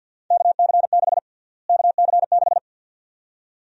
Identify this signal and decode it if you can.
Morse code, heard as a keyed tone, about 53 wpm, 700 Hz. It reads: KC6 KC6
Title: USA Call Sign Prefixes - Repeated 2x - 50wpm